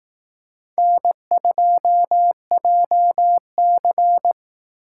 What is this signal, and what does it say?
Morse code, heard as a keyed tone, about 18 wpm, 700 Hz. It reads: N2JC
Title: Speed Racing - USA 1x2 Call Signs 15wpm